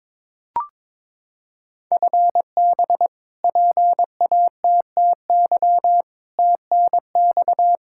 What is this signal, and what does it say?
Morse code, heard as a keyed tone, about 22 wpm, 700 Hz. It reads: FB PATTY TNX
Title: K1USN SST QSO Practice 22wpm